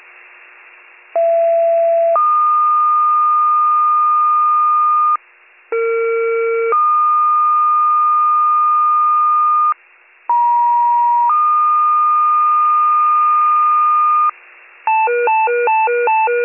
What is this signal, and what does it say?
Two-tone page, heard as a one-second tone, then a three-second tone, about 670 then 1220 Hz, about 460 then 1220 Hz, about 950 then 1220 Hz. It is heard from 1.2 to 14.3 s.